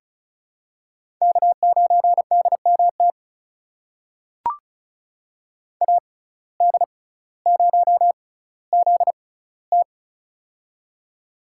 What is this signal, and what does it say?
Morse code, heard as a keyed tone, about 34 wpm, spaced as for 18 wpm, 700 Hz. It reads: K9DMT AD0ZT
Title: USA Call Signs - 5 Characters - 6x Character Spacing - 35wpm